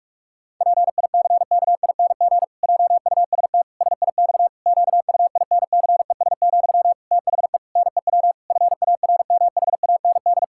Morse code, heard as a keyed tone, about 48 wpm, 700 Hz. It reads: PICKING JUST SIX QUINCES, THE NEW FARMHAND
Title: Pangram Challenge v2 - 45wpm